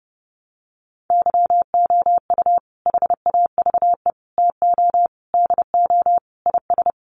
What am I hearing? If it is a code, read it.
Morse code, heard as a keyed tone, about 30 wpm, 700 Hz. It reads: YOU HAVE TO DO IS
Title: Sets of 5 Words 30wpm